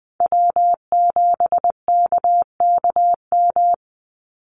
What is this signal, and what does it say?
Morse code, heard as a keyed tone, about 20 wpm, 700 Hz. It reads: W7KKM
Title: NA SOTA Chaser Callsigns 20wpm